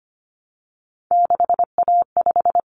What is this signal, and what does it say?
Morse code, heard as a keyed tone, about 25 wpm, 700 Hz. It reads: 6A5
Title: Common Cut Numbers - 3-Digit 25wpm